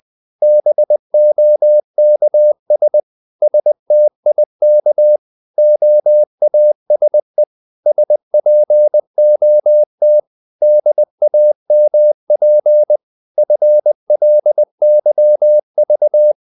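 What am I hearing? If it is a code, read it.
Morse code, heard as a keyed tone, about 20 wpm, 600 Hz. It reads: BOKS STIK OASE SPOT DAMP FLYV
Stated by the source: Morse code practice of short Danish words at 20 wpm, from CW med Gnister.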